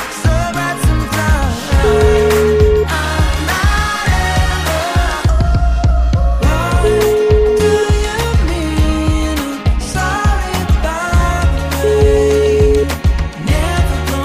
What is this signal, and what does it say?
Ringback tone (Europe, 425 Hz), with noise.